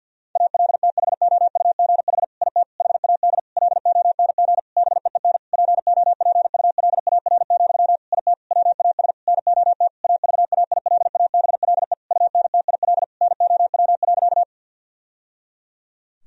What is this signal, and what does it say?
Morse code, heard as a keyed tone, about 52 wpm, 700 Hz. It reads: ALTHOUGH IT HAD LONG BEEN POPULAR, IT WAS NOT AVAILABLE UNTIL NOW.
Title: Sentences from Top 1000 Words - Review Part 3 - 50wpm